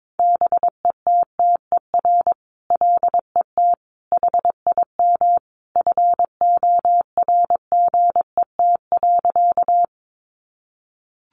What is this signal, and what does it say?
Morse code, heard as a keyed tone, about 22 wpm, 700 Hz. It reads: BETTER LET HIM FORGET.
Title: Sentences from Top 1200 Words 22wpm